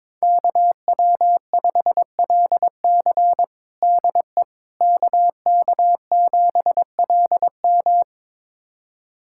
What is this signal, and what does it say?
Morse code, heard as a keyed tone, about 22 wpm, 700 Hz. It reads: KW5LC DE KK7LM
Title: USA Callsign DE Callsign - 5 Characters - 22wpm